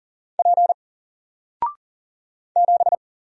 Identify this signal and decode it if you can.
Morse code, heard as a keyed tone, about 40 wpm, 700 Hz. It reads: P 7
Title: Single Letter-Number Alt 40wpm